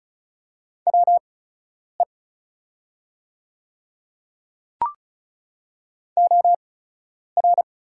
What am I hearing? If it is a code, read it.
Morse code, heard as a keyed tone, about 36 wpm, spaced as for 10 wpm, 690 Hz. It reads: WE OR
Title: 8x Character Spacing - Top 100 Words - 35wpm